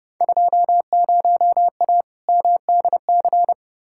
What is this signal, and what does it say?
Morse code, heard as a keyed tone, about 30 wpm, 700 Hz. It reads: 20A MDC
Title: Field Day Exchanges Encore - 30wpm